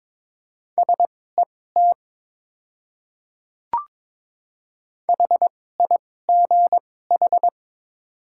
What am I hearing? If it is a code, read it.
Morse code, heard as a keyed tone, about 22 wpm, spaced as for 16 wpm, 700 Hz. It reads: SET HIGH
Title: 2x Character Spacing - Top 200 Words - 22wpm